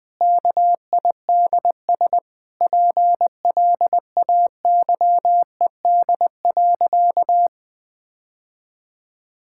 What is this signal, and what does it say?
Morse code, heard as a keyed tone, about 20 wpm, 700 Hz. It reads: KIDS PLAYED.
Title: Sentences with 3-Word Phrases - 20wpm